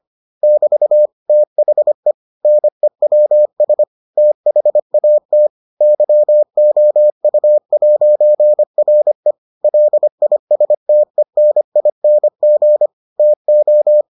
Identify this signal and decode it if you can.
Morse code, heard as a keyed tone, about 25 wpm, 600 Hz. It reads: <BT> THE NEWS THAT YOU'RE LISTENING TO